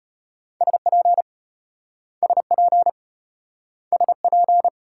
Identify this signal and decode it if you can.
Morse code, heard as a keyed tone, about 34 wpm, 700 Hz. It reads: SP SP SP